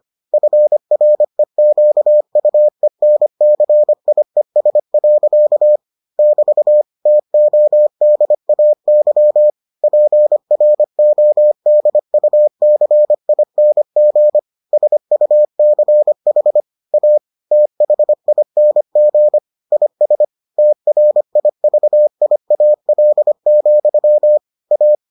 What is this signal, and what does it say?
Morse code, heard as a keyed tone, about 25 wpm, 600 Hz. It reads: FREQUENCIES. <BT> TODAY PRODUCING SUCH A THING IS TRIVIAL, A